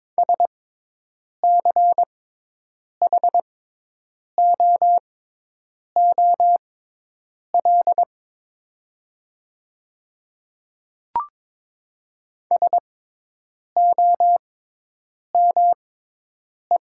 Morse code, heard as a keyed tone, about 22 wpm, spaced as for 8 wpm, 700 Hz. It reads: SCHOOL SOME
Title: 6x Character Spacing - Top 300 Words - 22wpm